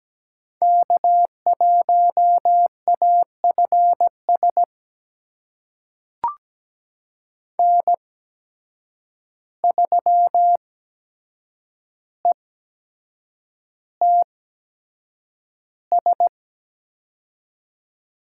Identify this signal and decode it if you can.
Morse code, heard as a keyed tone, about 17 wpm, spaced as for 7 wpm, 700 Hz. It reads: K1AFS N3ETS
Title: USA Call Signs - 5 Characters - 8x Character Spacing - 17wpm